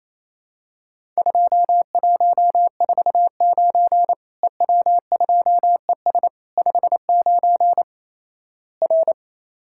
Morse code, heard as a keyed tone, about 27 wpm, 700 Hz. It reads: 2149 EW2ES 59 R
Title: Worked All Europe DX Contest - QTC Exchange - 28wpm